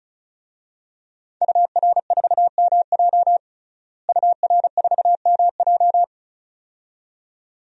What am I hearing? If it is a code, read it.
Morse code, heard as a keyed tone, about 36 wpm, 700 Hz. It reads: UR4MJ UR4MJ
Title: DXCC-100 Call Signs - 5 Characters - Repeated 2x - 35wpm